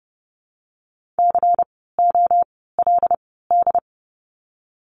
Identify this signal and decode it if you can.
Morse code, heard as a keyed tone, about 30 wpm, spaced as for 17 wpm, 700 Hz. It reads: COLD
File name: Lesson 16f - Letter c - Words 30wpm